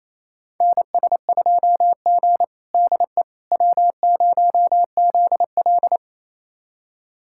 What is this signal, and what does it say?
Morse code, heard as a keyed tone, about 28 wpm, 700 Hz. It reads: NS2G DE W0ZL
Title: USA Callsign DE Callsign - 4 Characters - 28wpm